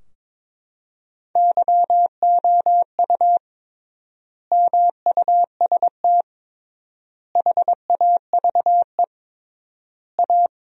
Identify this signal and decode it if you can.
Morse code, heard as a keyed tone, about 22 wpm, 700 Hz. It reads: YOU MUST HAVE A
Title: 3x Word-Spacing - Sentences from Top 500 Words - 22wpm